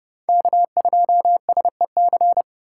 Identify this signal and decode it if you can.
Morse code, heard as a keyed tone, about 30 wpm, 700 Hz. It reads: K2SEC